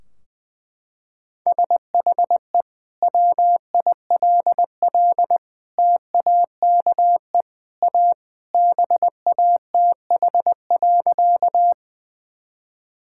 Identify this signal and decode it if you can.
Morse code, heard as a keyed tone, about 21 wpm, 700 Hz. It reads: SHE WILL TAKE A BATH.
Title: Sentences from Top 1600 Words 20wpm